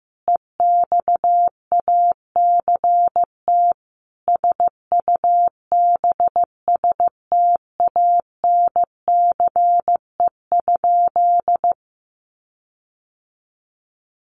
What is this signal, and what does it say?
Morse code, heard as a keyed tone, about 15 wpm, 700 Hz. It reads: EXACT SUBSTANCE?
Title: Sentences from Top 1500 Words 15wpm